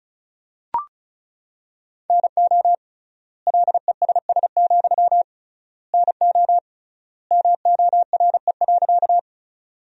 Morse code, heard as a keyed tone, about 36 wpm, 700 Hz. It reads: NO LESS, NO MORE.
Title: 3x Word-Spacing - Sentences from Top 400 Words - 35wpm